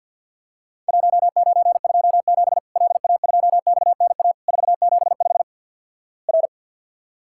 Morse code, heard as a keyed tone, about 51 wpm, 700 Hz. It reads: 1927 LA2XNA 475 R
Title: Worked All Europe DX Contest - QTC Exchange - 50wpm